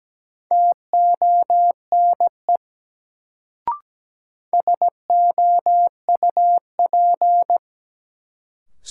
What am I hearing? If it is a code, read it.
Morse code, heard as a keyed tone, about 17 wpm, 700 Hz. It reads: TONE SOUP
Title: Sentences from Top 1600 Words - New Word Drill Alt - 17wpm